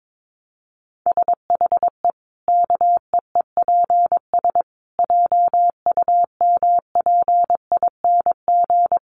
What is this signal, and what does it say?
Morse code, heard as a keyed tone, about 22 wpm, 700 Hz. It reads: SHE KEEPS JUMPING
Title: Easy Phrases 22wpm